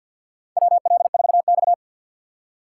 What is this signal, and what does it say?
Morse code, heard as a keyed tone, about 53 wpm, 700 Hz. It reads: WL4X
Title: USA Call Signs - 4 Characters - 2x Character Spacing - 50wpm